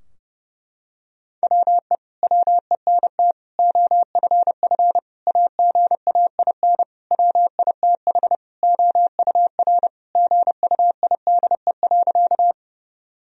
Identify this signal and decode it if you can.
Morse code, heard as a keyed tone, about 32 wpm, 700 Hz. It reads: WE WENT OFF AGAIN WITH OUR GUIDE.
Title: Sentences from Top 1000 Words - Review Part 3 - 30wpm